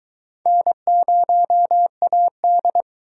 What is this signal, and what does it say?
Morse code, heard as a keyed tone, about 21 wpm, 700 Hz. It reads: N0AD